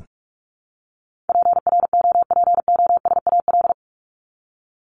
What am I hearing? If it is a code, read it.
Morse code, heard as a keyed tone, about 44 wpm, 700 Hz. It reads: PROPOSAL
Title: Top 1751-2000 Words 45wpm